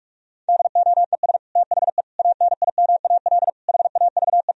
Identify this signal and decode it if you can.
Morse code, heard as a keyed tone, about 46 wpm, 700 Hz. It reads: DOES THE ANIMAL HAVE
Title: Sentences from Top 1000 Words - Review Part 1 - 45wpm